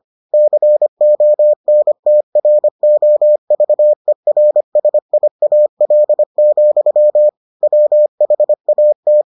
Morse code, heard as a keyed tone, about 25 wpm, 600 Hz. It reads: CONTROVERSIAL, WHAT